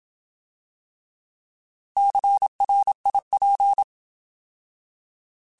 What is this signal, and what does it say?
Morse code, heard as a keyed tone, about 27 wpm, 800 Hz. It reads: CRIP
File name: Alphabetical / Alphabetical mp3s / Cs @25